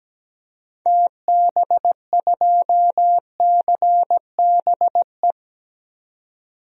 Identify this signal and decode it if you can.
Morse code, heard as a keyed tone, about 17 wpm, 700 Hz. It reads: TB2CBE
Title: DXCC-100 Call Signs - 6 Characters - 17wpm